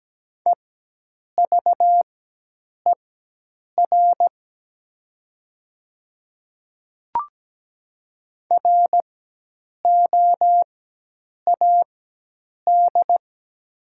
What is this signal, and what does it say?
Morse code, heard as a keyed tone, about 17 wpm, spaced as for 8 wpm, 700 Hz. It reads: EVER ROAD
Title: 4x Character Spacing - Top 500 Words - 17wpm